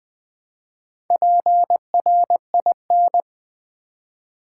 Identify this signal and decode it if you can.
Morse code, heard as a keyed tone, about 20 wpm, 700 Hz. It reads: PRIN
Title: Build A Word - 2 Letters At A Time - 8-Letter Words - 20wpm